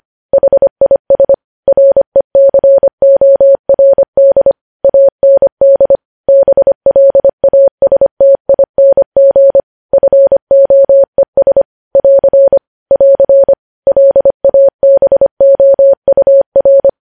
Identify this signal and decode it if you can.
Morse code, heard as a keyed tone, about 25 wpm, 550 Hz. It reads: HIS RECORD AND BLASTING FOES <AR> <AR> LABOUR
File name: podcast / News headlines-1x-25x25